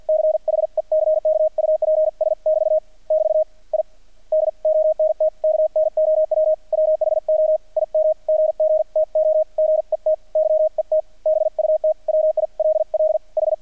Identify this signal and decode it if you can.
Morse code, heard as a keyed tone, about 48 wpm, 620 Hz. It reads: CHECKUPS <BT> <BT> I DONT KNOW WHO IM GG TO GET YET BUT WILL H
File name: archived / CRSnet